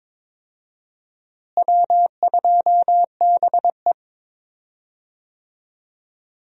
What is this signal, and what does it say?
Morse code, heard as a keyed tone, about 22 wpm, 700 Hz. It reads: W2BE
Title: USA Call Signs - 4 Characters - Repeated 2x - 22wpm